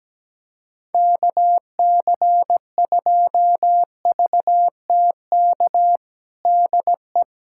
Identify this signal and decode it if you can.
Morse code, heard as a keyed tone, about 17 wpm, 700 Hz. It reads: KC2VTK DE
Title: USA Callsign DE Callsign 17wpm